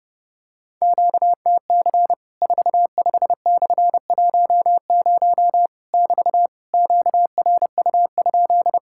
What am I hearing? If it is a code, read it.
Morse code, heard as a keyed tone, about 30 wpm, 700 Hz. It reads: QTC 45/10 <BT> QRU?
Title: Worked All Europe DX Contest - QTC Exchange - 30wpm